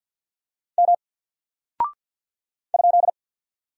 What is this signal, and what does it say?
Morse code, heard as a keyed tone, about 50 wpm, 700 Hz. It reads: M ?
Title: Single Character (Letter:Number:Punctuation:Prosign) 50wpm